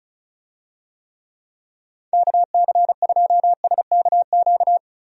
Morse code, heard as a keyed tone, about 34 wpm, 700 Hz. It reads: KC2SKQ
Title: USA Call Signs - 6 Characters - 35wpm